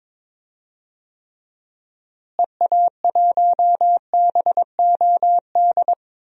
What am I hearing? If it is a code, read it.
Morse code, heard as a keyed tone, about 22 wpm, 700 Hz. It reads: EA1BOD
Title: DXCC-100 Call Signs - 6 Characters - Repeated 2x - 22wpm